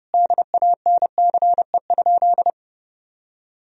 Morse code, heard as a keyed tone, about 32 wpm, 700 Hz. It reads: DANCE?
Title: Sentences from Top 1000 Words - Review Part 4 - 30wpm